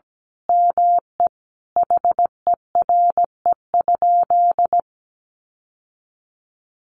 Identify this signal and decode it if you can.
Morse code, heard as a keyed tone, about 17 wpm, 700 Hz. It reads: ME HERE?
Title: Sentences from Top 1200 Words 17wpm